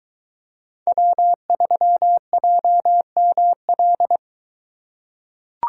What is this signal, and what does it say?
Morse code, heard as a keyed tone, about 23 wpm, 700 Hz. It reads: W3JML